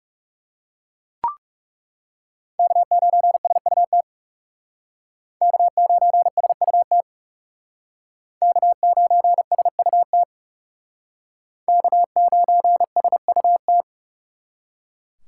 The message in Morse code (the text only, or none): K9SUT K9SUT K9SUT K9SUT